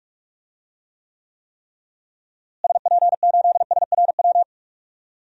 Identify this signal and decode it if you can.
Morse code, heard as a keyed tone, about 44 wpm, 700 Hz. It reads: SP8SRW
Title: DXCC-100 Call Signs - 6 Characters - Repeated 3x - 45wpm